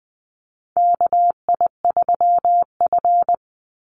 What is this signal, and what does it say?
Morse code, heard as a keyed tone, about 20 wpm, 700 Hz. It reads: KI3F
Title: Call Signs Repeated 2x - 20wpm